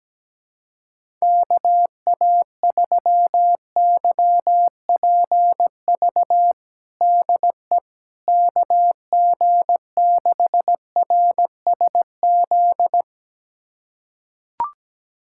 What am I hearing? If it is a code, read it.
Morse code, heard as a keyed tone, about 17 wpm, 700 Hz. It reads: KA3YPV DE KG6RSZ
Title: USA Callsign DE Callsign 17wpm